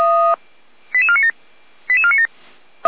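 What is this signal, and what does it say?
Five-tone selective call (ZVEI), heard as quick runs of five tones, tones of about 70 ms. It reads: 88497 88497
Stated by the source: ZVEI five-tone selective call received on an SDR.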